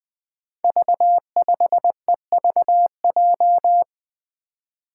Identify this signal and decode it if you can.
Morse code, heard as a keyed tone, about 20 wpm, 700 Hz. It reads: V5EVJ
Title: Five Challenging Letter-Number Combinations 20wpm